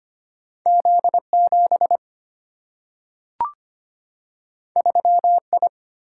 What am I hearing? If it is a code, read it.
Morse code, heard as a keyed tone, about 25 wpm, 700 Hz. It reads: Z 7 3 I
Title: Single-Letter Single-Number 25wpm